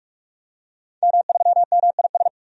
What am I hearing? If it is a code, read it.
Morse code, heard as a keyed tone, about 46 wpm, 700 Hz. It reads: M3MIS
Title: DXCC-100 Call Signs - 5 Characters - 45wpm